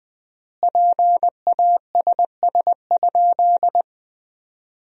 Morse code, heard as a keyed tone, about 20 wpm, 700 Hz. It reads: PASS?
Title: 3x Word-Spacing - Sentences from Top 400 Words - 20wpm